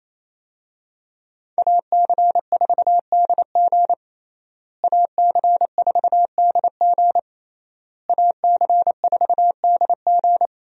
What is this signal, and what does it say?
Morse code, heard as a keyed tone, about 28 wpm, 700 Hz. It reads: AC4DG AC4DG AC4DG
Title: USA Call Signs - 5 Characters - Repeated 3x - 28wpm